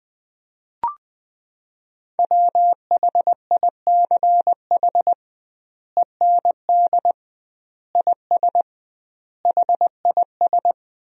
Morse code, heard as a keyed tone, about 20 wpm, 700 Hz. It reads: WHICH END IS HIS
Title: 2x Word-Spacing - Sentences from Top 300 Words - 20wpm